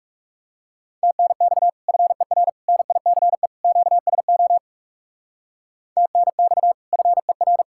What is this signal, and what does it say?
Morse code, heard as a keyed tone, about 39 wpm, 700 Hz. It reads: TNX FER NICE QSO TNX FER